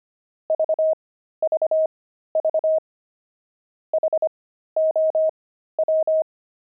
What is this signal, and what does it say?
Morse code, heard as a keyed tone, about 25 wpm, spaced as for 13 wpm, 630 Hz. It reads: VVV HOW
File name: pangram-fs-13-cs-25-frogs